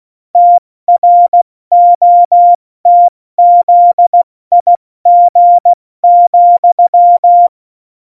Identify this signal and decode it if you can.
Morse code, heard as a keyed tone, about 16 wpm, 700 Hz. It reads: TROTZIG,
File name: Morsekurs